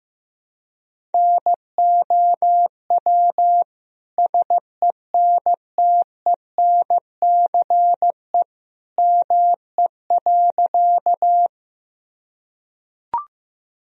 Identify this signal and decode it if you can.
Morse code, heard as a keyed tone, about 16 wpm, 700 Hz. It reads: NOW SENTENCE ME.
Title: Sentences from Top 1000 Words - Review Part 1 - 15wpm